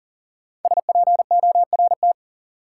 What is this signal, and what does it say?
Morse code, heard as a keyed tone, about 40 wpm, 700 Hz. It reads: SPORT
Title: Sentences from Top 1000 Words - Review Part 4 - New Word Drill - 40wpm